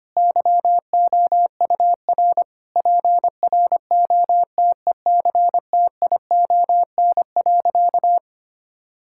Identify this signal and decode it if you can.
Morse code, heard as a keyed tone, about 25 wpm, 700 Hz. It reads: YOUR PROTECTION.
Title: Sentences from Top 1600 Words 25wpm